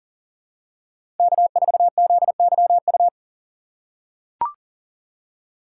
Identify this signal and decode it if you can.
Morse code, heard as a keyed tone, about 40 wpm, 700 Hz. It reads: K4ZYU
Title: USA Call Signs - 5 Characters - 4x Character Spacing - 40wpm